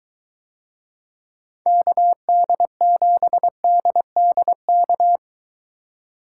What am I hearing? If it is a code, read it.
Morse code, heard as a keyed tone, about 21 wpm, 700 Hz. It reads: KD7DDK